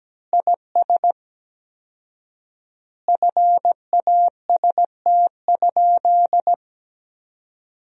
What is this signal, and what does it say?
Morse code, heard as a keyed tone, about 17 wpm, 700 Hz. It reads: IS FAST?
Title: 4x Word-Spacing - Sentences from Top 400 Words - 17wpm